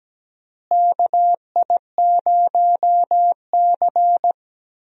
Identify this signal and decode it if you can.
Morse code, heard as a keyed tone, about 17 wpm, 700 Hz. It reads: KI0C
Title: USA Call Signs - 4 Characters - Repeated 2x - 17wpm